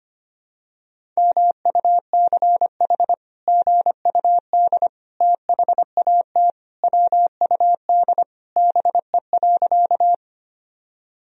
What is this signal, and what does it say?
Morse code, heard as a keyed tone, about 25 wpm, 700 Hz. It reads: MUCH GUD THAT WUD BE.
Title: Sentences from Top 200 Words and Abbreviations 25wpm